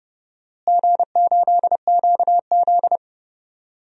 Morse code, heard as a keyed tone, about 30 wpm, 700 Hz. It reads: G8QZ
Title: DXCC-100 Call Signs - 4 Characters - Repeated 2x - 30wpm